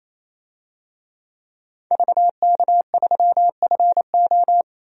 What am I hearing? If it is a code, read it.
Morse code, heard as a keyed tone, about 28 wpm, 700 Hz. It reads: VK3FO
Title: DXCC-100 Call Signs - 5 Characters - 28wpm